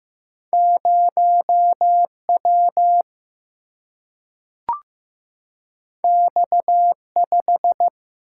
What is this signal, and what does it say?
Morse code, heard as a keyed tone, about 15 wpm, 700 Hz. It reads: 0 W X 5
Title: Single-Letter Single-Number 15wpm